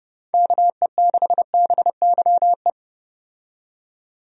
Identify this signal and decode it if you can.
Morse code, heard as a keyed tone, about 30 wpm, 700 Hz. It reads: KE6BYE
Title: USA Call Signs - 6 Characters - 30wpm